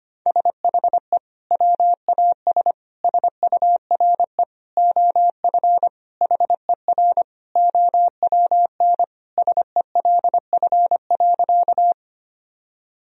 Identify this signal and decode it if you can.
Morse code, heard as a keyed tone, about 25 wpm, 700 Hz. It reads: SHE WAS SURE OF HER OWN SELF.
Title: Sentences from Top 1000 Words - Review Part 1 - 25wpm